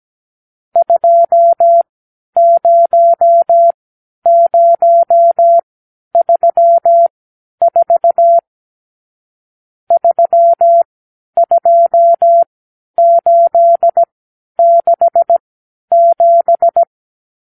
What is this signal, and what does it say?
Morse code, heard as a keyed tone, about 17 wpm, spaced as for 10 wpm, 680 Hz. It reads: 20034 32867
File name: arrl / Review (1-0)